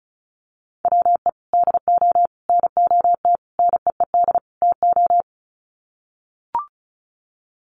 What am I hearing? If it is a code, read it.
Morse code, heard as a keyed tone, about 34 wpm, 700 Hz. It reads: WE DO NOT NEED TO
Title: Sets of 5 Words Encore 35wpm